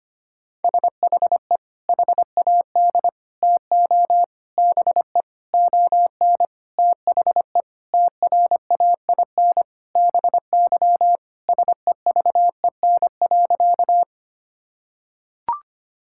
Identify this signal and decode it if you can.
Morse code, heard as a keyed tone, about 25 wpm, 700 Hz. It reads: SHE HAD TO BE ON THE TRAIN BY SEVEN.
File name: practice / 25wpm / Sentences with 4-Word Phrases - 25wpm